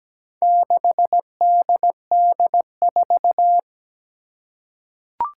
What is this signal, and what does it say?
Morse code, heard as a keyed tone, about 17 wpm, 700 Hz. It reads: 6DD4
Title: Four Challenging Letter-Number Combinations 17wpm